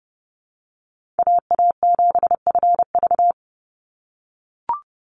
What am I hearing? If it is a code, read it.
Morse code, heard as a keyed tone, about 30 wpm, 700 Hz. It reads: AA7FV
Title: Call Signs - 30wpm